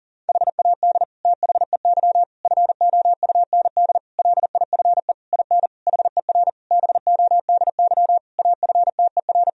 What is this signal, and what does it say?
Morse code, heard as a keyed tone, about 40 wpm, 700 Hz. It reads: HAD THEY FOUND LIFE IN HER BODY AFTER
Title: Sentences from Top 1000 Words - Review Part 2 - 40wpm